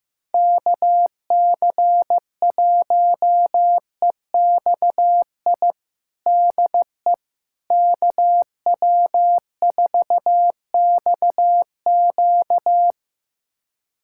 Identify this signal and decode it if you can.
Morse code, heard as a keyed tone, about 15 wpm, 700 Hz. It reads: KC1EXI DE KW4XQ
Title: USA Callsign DE Callsign 15wpm